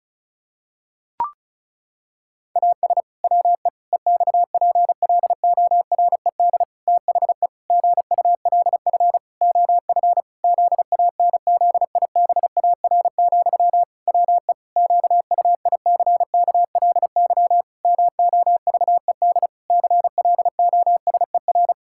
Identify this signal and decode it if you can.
Morse code, heard as a keyed tone, about 34 wpm, 700 Hz. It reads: AS WE EXPLORED THE GULF OF ZANZIBAR, WE QUICKLY MOVED CLOSER
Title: Pangram Challenge v2 - 35wpm